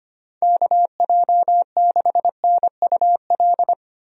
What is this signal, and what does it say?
Morse code, heard as a keyed tone, about 25 wpm, 700 Hz. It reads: KJ6NUL
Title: USA Call Signs - 6 Characters - 25wpm